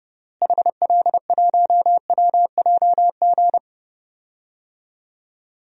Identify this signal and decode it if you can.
Morse code, heard as a keyed tone, about 32 wpm, 700 Hz. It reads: HL1WJG
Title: DXCC-100 Call Signs - 6 Characters - Repeated 2x - 30wpm